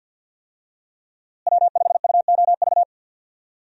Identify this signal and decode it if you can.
Morse code, heard as a keyed tone, about 52 wpm, 700 Hz. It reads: W5UOV
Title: USA Call Signs - 5 Characters - Repeated 2x - 50wpm